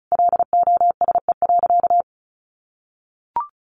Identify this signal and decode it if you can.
Morse code, heard as a keyed tone, about 34 wpm, 700 Hz. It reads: LOSE.
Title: Sentences from Top 1100 Words 35wpm